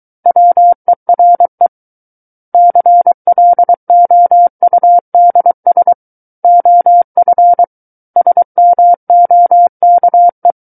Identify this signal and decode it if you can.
Morse code, heard as a keyed tone, about 23 wpm, 700 Hz. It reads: WERE CLOUDS OF SMOKE
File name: Book One / 14-Chapter 14 - 23 WPM